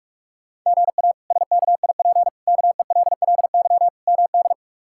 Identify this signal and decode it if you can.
Morse code, heard as a keyed tone, about 46 wpm, 700 Hz. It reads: GA SKIP KELLY MD
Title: K1USN SST QSO Practice - 45wpm